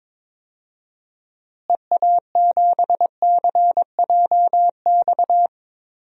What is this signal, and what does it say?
Morse code, heard as a keyed tone, about 22 wpm, 700 Hz. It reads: EA7CJX
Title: DXCC-100 Call Signs - 6 Characters - Repeated 3x - 22wpm